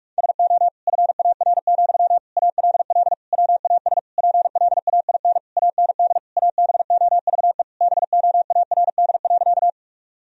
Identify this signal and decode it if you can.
Morse code, heard as a keyed tone, about 46 wpm, 700 Hz. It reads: SO FAR, ALL WAS PLAIN AND ABOVE BOARD.
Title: Sentences from Top 1000 Words - Review Part 4 - 45wpm